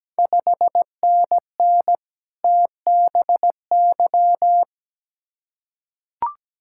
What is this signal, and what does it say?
Morse code, heard as a keyed tone, about 17 wpm, 690 Hz. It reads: 5NN TBY
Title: Ontario QSO Party Exchange 17wpm